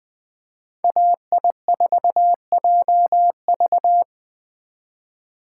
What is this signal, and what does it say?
Morse code, heard as a keyed tone, about 21 wpm, 700 Hz. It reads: AI4JV